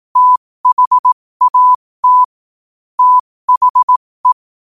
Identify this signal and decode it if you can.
Morse code, heard as a keyed tone, about 18 wpm, spaced as for 15 wpm, 1000 Hz. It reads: THAT THE
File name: qotdfiles / 15wpm